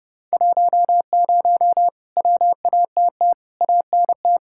Morse code, heard as a keyed tone, about 29 wpm, 700 Hz. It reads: 10 WATT ANT